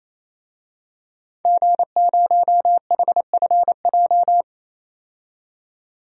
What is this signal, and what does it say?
Morse code, heard as a keyed tone, about 28 wpm, 700 Hz. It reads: G0HFJ
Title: DXCC-100 Call Signs - 5 Characters - 28wpm